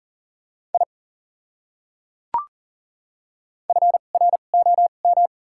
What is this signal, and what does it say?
Morse code, heard as a keyed tone, about 37 wpm, spaced as for 28 wpm, 700 Hz. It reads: I FROM
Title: 2x Character Spacing - Top 300 Words - 40wpm